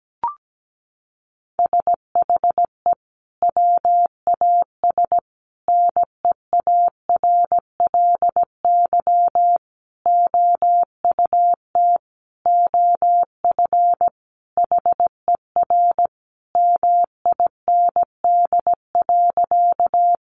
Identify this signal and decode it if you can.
Morse code, heard as a keyed tone, about 17 wpm, 700 Hz. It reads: SHE WAS NEARLY OUT OF HER MIND.
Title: Sentences from Top 1300 Words 17wpm